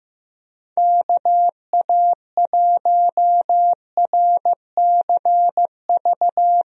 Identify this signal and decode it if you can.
Morse code, heard as a keyed tone, about 15 wpm, 700 Hz. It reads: KA1RCV
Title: USA Call Signs - 6 Characters - Repeated 2x - 15wpm